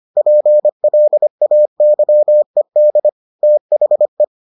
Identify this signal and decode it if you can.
Morse code, heard as a keyed tone, about 25 wpm, 600 Hz. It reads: PLAYED THE